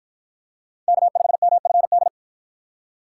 Morse code, heard as a keyed tone, about 53 wpm, 700 Hz. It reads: K5MFD